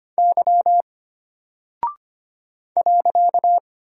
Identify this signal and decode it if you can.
Morse code, heard as a keyed tone, about 25 wpm, 700 Hz. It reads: Y .
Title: Single Character (Letter:Number:Punctuation:Prosign) Alt 25wpm